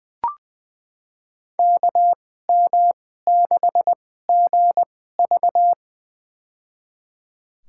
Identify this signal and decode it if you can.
Morse code, heard as a keyed tone, about 20 wpm, spaced as for 14 wpm, 700 Hz. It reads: KM6GV
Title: USA Call Signs - 5 Characters - 2x Character Spacing - 20wpm